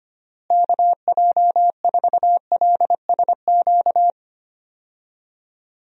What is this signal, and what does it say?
Morse code, heard as a keyed tone, about 25 wpm, 700 Hz. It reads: KJ4LSQ
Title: USA Call Signs - 6 Characters - Repeated 2x - 25wpm